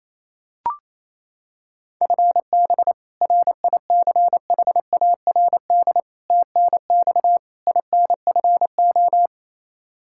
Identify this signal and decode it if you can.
Morse code, heard as a keyed tone, about 27 wpm, 700 Hz. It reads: FB RICHARD TNX INFO
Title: Practice QSO - Exchange 3 - More Info - 28wpm